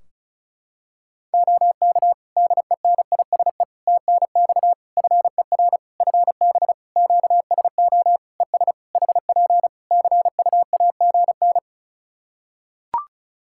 Morse code, heard as a keyed tone, about 34 wpm, 700 Hz. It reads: OK DENISE TNX FER FB QSO ES HP CUAGN
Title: Practice QSO - Exchange 4 - Ending - 35wpm